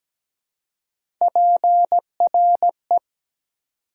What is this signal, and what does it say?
Morse code, heard as a keyed tone, about 17 wpm, 700 Hz. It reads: PRE
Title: Prefixes and Suffixes 17wpm